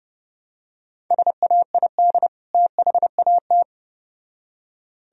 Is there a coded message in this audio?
Yes: Morse code from 1.1 to 3.6 s, heard as a keyed tone, about 29 wpm, 700 Hz. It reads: SAID THAT